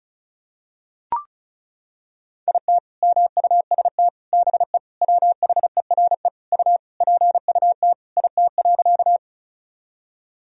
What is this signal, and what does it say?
Morse code, heard as a keyed tone, about 36 wpm, 700 Hz. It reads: IT MUST BE WHERE U PUT IT.
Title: Sentences from Top 200 Words and Abbreviations 35wpm